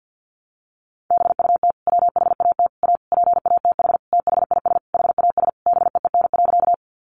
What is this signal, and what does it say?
Morse code, heard as a keyed tone, about 50 wpm, 700 Hz. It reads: BUT WHAT A PATH THIS HAS BEEN.
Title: Sentences from Top 1400 Words 50wpm